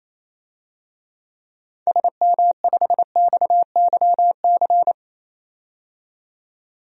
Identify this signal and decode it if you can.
Morse code, heard as a keyed tone, about 28 wpm, 700 Hz. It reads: SM5XYC
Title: DXCC-100 Call Signs - 6 Characters - 28wpm